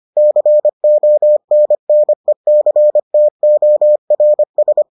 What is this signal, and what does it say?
Morse code, heard as a keyed tone, about 25 wpm, 600 Hz. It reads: CONNECTORS